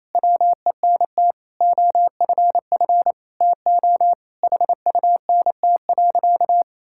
Morse code, heard as a keyed tone, about 28 wpm, 700 Hz. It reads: WENT OFF TO HUNT.
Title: Sentences from Top 1600 Words 28wpm